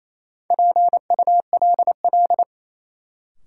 Morse code, heard as a keyed tone, about 28 wpm, 700 Hz. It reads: PULL
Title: Sentences from Top 400 Words - New Word Drill - 28wpm